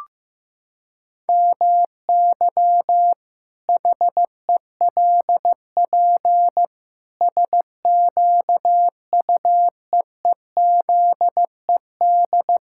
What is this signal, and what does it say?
Morse code, heard as a keyed tone, about 16 wpm, 700 Hz. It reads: MY HELP SQUEEZED
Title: Pangram Challenge v2 - 15wpm